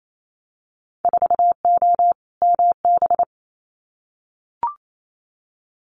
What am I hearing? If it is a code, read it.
Morse code, heard as a keyed tone, about 28 wpm, 700 Hz. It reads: 4O MB
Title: Winter Field Day Exchanges Encore 28wpm